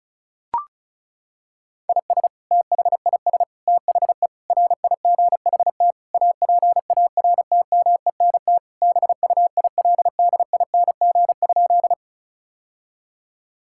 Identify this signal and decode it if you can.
Morse code, heard as a keyed tone, about 36 wpm, 700 Hz. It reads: IS THIS THE RIGHT APARTMENT BUILDING?
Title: Sentences from Top 1000 Words - Review Part 6 - 35wpm